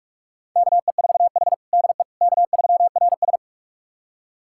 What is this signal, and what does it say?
Morse code, heard as a keyed tone, about 48 wpm, 700 Hz. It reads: KE4H DE K3RS
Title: USA Callsign DE Callsign - 4 Characters - 45wpm